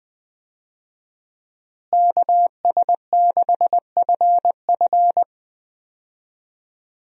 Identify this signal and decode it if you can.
Morse code, heard as a keyed tone, about 20 wpm, 700 Hz. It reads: KS6FF
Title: USA Call Signs - 5 Characters - Repeated 2x - 20wpm